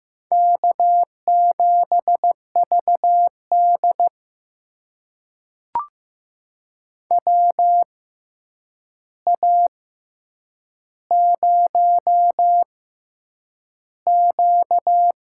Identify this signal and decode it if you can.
Morse code, heard as a keyed tone, about 15 wpm, spaced as for 8 wpm, 700 Hz. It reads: K7VD WA0Q
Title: USA Call Signs - 4 Characters - 6x Character Spacing - 15wpm